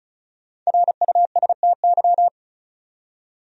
Morse code, heard as a keyed tone, about 34 wpm, 700 Hz. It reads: RUSTY
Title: USA Common Names - 35wpm